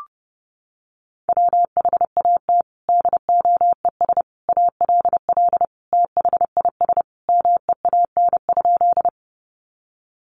Morse code, heard as a keyed tone, about 30 wpm, 700 Hz. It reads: WHAT DOES ALL THIS MEAN?
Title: Sentences from Top 200 Words 30wpm